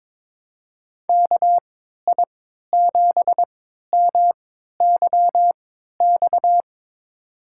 Morse code, heard as a keyed tone, about 21 wpm, spaced as for 12 wpm, 700 Hz. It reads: KI7MYX
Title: USA Call Signs - 6 Characters - 3x Character Spacing - 22wpm